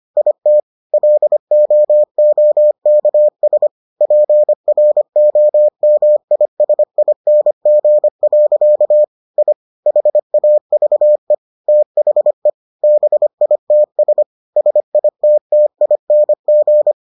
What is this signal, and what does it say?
Morse code, heard as a keyed tone, about 25 wpm, 600 Hz. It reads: IT LOOKS PROMISING. I HAVE THE BITS SITTING